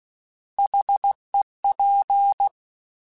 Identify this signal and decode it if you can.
Morse code, heard as a keyed tone, about 16 wpm, 800 Hz. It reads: HEP
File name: Alphabetical / Alphabetical mp3s / Hs @15